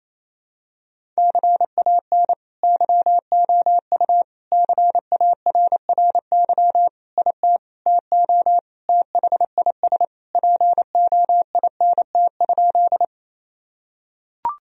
Morse code, heard as a keyed tone, about 28 wpm, 700 Hz. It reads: CAN YOU CARRY IT TO THIS POINT?
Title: Sentences from Top 1000 Words - Review Part 1 - 28wpm